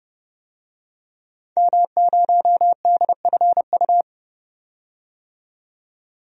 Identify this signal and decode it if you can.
Morse code, heard as a keyed tone, about 32 wpm, 700 Hz. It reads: M0DFU
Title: DXCC-100 Call Signs - 5 Characters - 30wpm